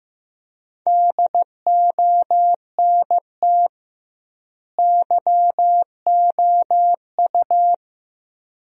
Morse code, heard as a keyed tone, about 16 wpm, 700 Hz. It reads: DONT YOU
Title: 2x Word-Spacing - Sentences from Top 300 Words - 15wpm